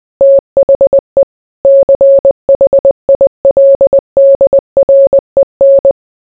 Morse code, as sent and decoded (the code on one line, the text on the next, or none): - .... / -.-. .... .. .-.. -.. .-. -.
THE CHILDREN